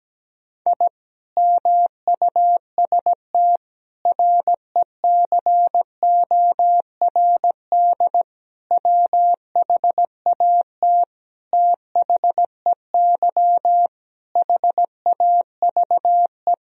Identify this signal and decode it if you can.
Morse code, heard as a keyed tone, about 17 wpm, 700 Hz. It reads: I MUST RECORD WHAT THEY HAVE
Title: Sentences from Top 1000 Words - Review Part 3 - 17wpm